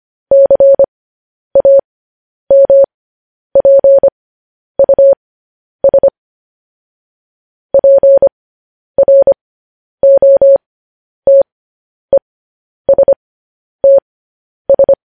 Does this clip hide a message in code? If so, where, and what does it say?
Yes: Morse code from 0.3 to 14.9 s, heard as a keyed tone, about 25 wpm, spaced as for 10 wpm, 550 Hz. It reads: CAMPUS PROTESTS